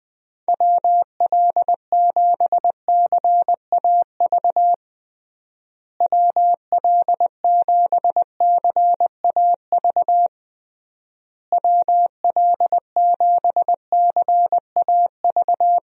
Morse code, heard as a keyed tone, about 21 wpm, 700 Hz. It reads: WL7CAV WL7CAV WL7CAV
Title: USA Call Signs - 6 Characters - Repeated 3x - 20wpm